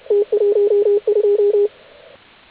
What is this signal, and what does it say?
Morse code, heard as a keyed tone, about 31 wpm, 410 Hz, with radio noise. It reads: T12